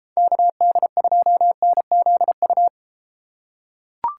Morse code, heard as a keyed tone, about 32 wpm, 700 Hz. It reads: KD2NZU